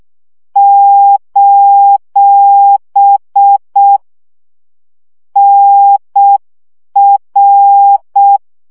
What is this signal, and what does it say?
Morse code, heard as a keyed tone, about 6 wpm, 800 Hz. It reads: : NR
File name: morze vizsga 2011-2 5